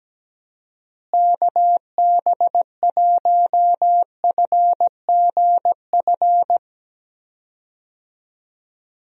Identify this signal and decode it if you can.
Morse code, heard as a keyed tone, about 17 wpm, 700 Hz. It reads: KB1FGF
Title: USA Call Signs - 6 Characters - Repeated 2x - 17wpm